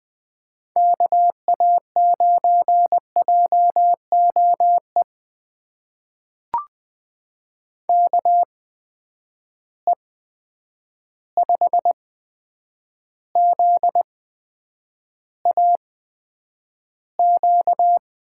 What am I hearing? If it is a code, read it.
Morse code, heard as a keyed tone, about 20 wpm, spaced as for 9 wpm, 700 Hz. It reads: KA9JOE KE5ZAQ
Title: USA Call Signs - 6 Characters - 8x Character Spacing - 20wpm